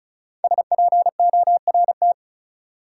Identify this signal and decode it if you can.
Morse code, heard as a keyed tone, about 36 wpm, 700 Hz. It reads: SPORT
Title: Sentences from Top 1500 Words - New Word Drill Alt - 35wpm